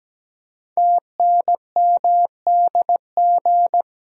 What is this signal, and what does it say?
Morse code, heard as a keyed tone, about 17 wpm, 700 Hz. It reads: TNMDG
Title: Code Groups - 5 Letters - 17wpm